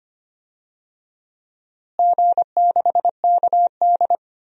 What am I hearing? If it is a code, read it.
Morse code, heard as a keyed tone, about 25 wpm, 700 Hz. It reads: G6KD
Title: DXCC-100 Call Signs - 4 Characters - Repeated 3x - 25wpm